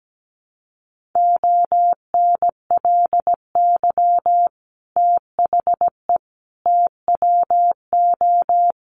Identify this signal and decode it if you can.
Morse code, heard as a keyed tone, about 17 wpm, 700 Hz. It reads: ONLY THE TWO
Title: Sentences from Top 800 Words 17wpm